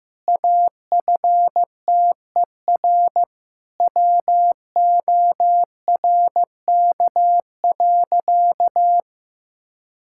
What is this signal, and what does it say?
Morse code, heard as a keyed tone, about 15 wpm, 700 Hz. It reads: AFTER WORK.
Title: Sentences from Top 1000 Words - Review Part 5 - 15wpm